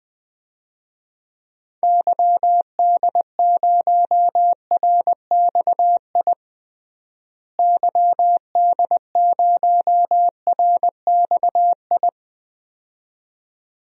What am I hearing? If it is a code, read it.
Morse code, heard as a keyed tone, about 21 wpm, 700 Hz. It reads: YD0RXI YD0RXI
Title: DXCC-100 Call Signs - 6 Characters - Repeated 2x - 20wpm